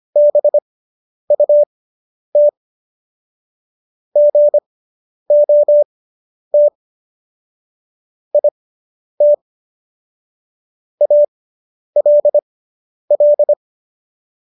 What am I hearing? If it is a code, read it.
Morse code, heard as a keyed tone, about 25 wpm, spaced as for 10 wpm, 590 Hz. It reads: BUT GOT IT ALL